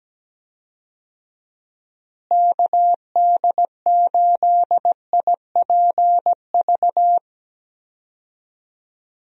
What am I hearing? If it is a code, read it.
Morse code, heard as a keyed tone, about 17 wpm, 700 Hz. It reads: KD8IPV
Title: USA Call Signs - 6 Characters - Repeated 2x - 17wpm